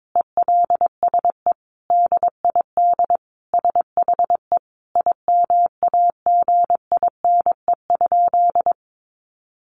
Morse code, heard as a keyed tone, about 22 wpm, 700 Hz. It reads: ELSE DID SHE IMAGINE?
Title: Sentences from Top 1400 Words 22wpm